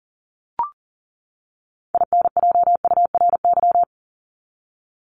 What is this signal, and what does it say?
Morse code, heard as a keyed tone, about 40 wpm, 700 Hz. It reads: INJURY
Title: Top 1751-2000 Words 40wpm